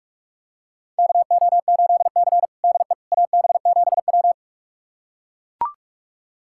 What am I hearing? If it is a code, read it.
Morse code, heard as a keyed tone, about 41 wpm, 700 Hz. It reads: KO8C DE AB7W
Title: USA Callsign DE Callsign - 4 Characters - 45wpm